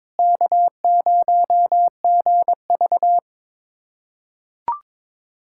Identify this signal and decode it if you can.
Morse code, heard as a keyed tone, about 22 wpm, 700 Hz. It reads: K0GV
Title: K1USN SST QSO Practice - 22wpm